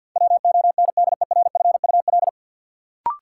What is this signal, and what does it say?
Morse code, heard as a keyed tone, about 49 wpm, 700 Hz. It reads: WONDERFUL
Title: Sentences from Top 1100 Words - New Word Drill Alt - 50wpm